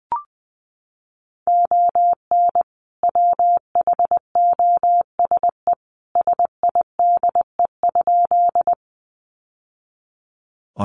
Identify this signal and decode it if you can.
Morse code, heard as a keyed tone, about 20 wpm, 700 Hz. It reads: ON WHOSE SIDE?
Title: Sentences from Top 1200 Words 20wpm